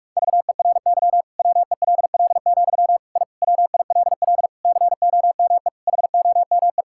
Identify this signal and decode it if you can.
Morse code, heard as a keyed tone, about 41 wpm, 700 Hz. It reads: VERY WELL, I WILL COME HOME
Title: Sentences from Top 1000 Words - Review Part 5 - 45wpm